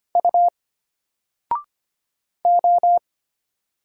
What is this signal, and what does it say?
Morse code, heard as a keyed tone, about 25 wpm, 700 Hz. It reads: U O
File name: Single-Letter-Number-25wpm